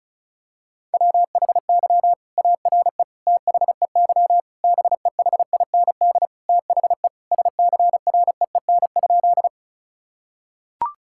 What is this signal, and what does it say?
Morse code, heard as a keyed tone, about 36 wpm, 700 Hz. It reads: WHY ARE THEY BEHIND THE SCREEN?
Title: Sentences from Top 1000 Words - Review Part 6 - 35wpm